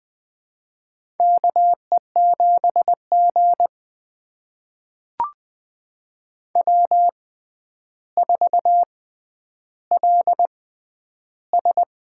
Morse code, heard as a keyed tone, about 20 wpm, spaced as for 10 wpm, 700 Hz. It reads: KE7G W4LS
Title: USA Call Signs - 6x Character Spacing - 20wpm